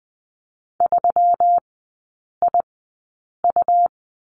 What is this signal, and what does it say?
Morse code, heard as a keyed tone, about 20 wpm, spaced as for 8 wpm, 700 Hz. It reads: 3IU
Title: 138 Lesson 23e - Letter f - Three Characters 20wpm